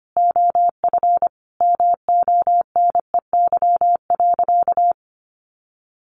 Morse code, heard as a keyed tone, about 25 wpm, 700 Hz. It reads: OF MONEY.
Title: Sentences from Top 1300 Words 25wpm